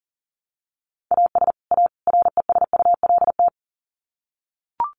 Morse code, heard as a keyed tone, about 40 wpm, 700 Hz. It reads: AS A RESULT
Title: Sets of 3 Words 40wpm